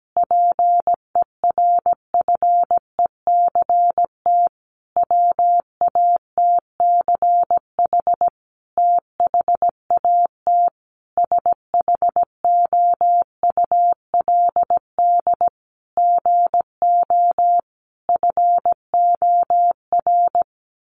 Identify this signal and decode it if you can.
Morse code, heard as a keyed tone, about 17 wpm, 700 Hz. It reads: PERFECT WATCH THAT SHOULD GO FOR
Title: Sentences from Top 1000 Words 17wpm